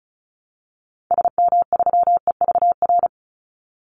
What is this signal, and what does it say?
Morse code, heard as a keyed tone, about 34 wpm, 700 Hz. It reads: SM3EVR
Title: Call Signs Repeated 2x - 35wpm